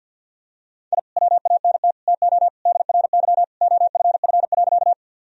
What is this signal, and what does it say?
Morse code, heard as a keyed tone, about 52 wpm, 700 Hz. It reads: I WANT TO DRY OFF.
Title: Sentences from Top 1000 Words - Review Part 3 - 50wpm